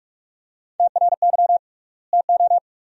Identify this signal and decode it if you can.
Morse code, heard as a keyed tone, about 47 wpm, 700 Hz. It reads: TRY TO
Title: 3x Word-Spacing - Sentences from Top 500 Words - 45wpm